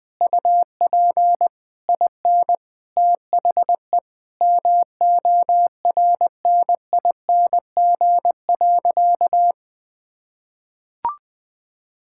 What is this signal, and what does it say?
Morse code, heard as a keyed tone, about 20 wpm, 700 Hz. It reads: UP IN THE MORNING.
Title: Sentences from Top 1000 Words - Review Part 5 - 20wpm